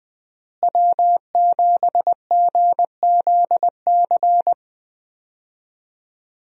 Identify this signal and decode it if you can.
Morse code, heard as a keyed tone, about 21 wpm, 700 Hz. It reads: W7GZC
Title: USA Call Signs - 5 Characters - Repeated 2x - 20wpm